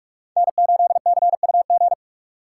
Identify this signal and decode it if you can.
Morse code, heard as a keyed tone, about 46 wpm, 690 Hz. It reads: N8CUG
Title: USA Call Signs - 4x Character Spacing - 45wpm